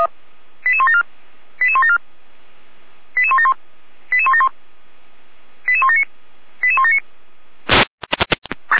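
Five-tone selective call (ZVEI), heard as quick runs of five tones, tones of about 70 ms. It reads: 88174 88174 88171 88171 88179 88179
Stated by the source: ZVEI five-tone selective call 88191, received by SDR.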